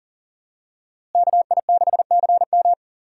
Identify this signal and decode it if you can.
Morse code, heard as a keyed tone, about 40 wpm, 700 Hz. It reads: KI6CM
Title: USA Call Signs - 5 Characters - Repeated 3x - 40wpm